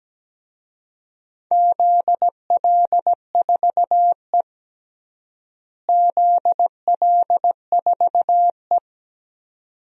Morse code, heard as a keyed tone, about 17 wpm, 700 Hz. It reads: ZL4E ZL4E
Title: DXCC-100 Call Signs - 4 Characters - Repeated 2x - 17wpm